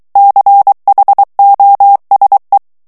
Morse code, heard as a keyed tone, about 23 wpm, 800 Hz. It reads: CHOSE
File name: ANGLAIS texte clair 3 vitesse 22 mots minute